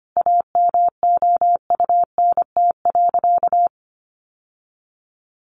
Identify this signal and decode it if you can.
Morse code, heard as a keyed tone, about 25 wpm, 700 Hz. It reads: AMOUNT.
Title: Sentences from Top 700 Words 25wpm